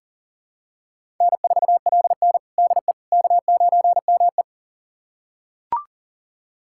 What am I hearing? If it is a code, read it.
Morse code, heard as a keyed tone, about 37 wpm, 700 Hz. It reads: N4LN DE K9ME
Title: USA Callsign DE Callsign - 4 Characters - 40wpm